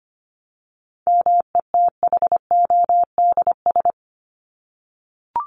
Morse code, heard as a keyed tone, about 25 wpm, 700 Hz. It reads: METHODS
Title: ES:S Suffix - Top 100 Words with Base Words 25wpm